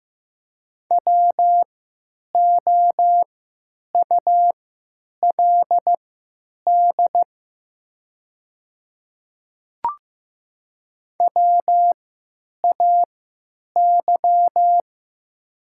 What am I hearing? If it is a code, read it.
Morse code, heard as a keyed tone, about 15 wpm, spaced as for 9 wpm, 700 Hz. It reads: WOULD WAY
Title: 3x Character Spacing - Top 100 Words - 15wpm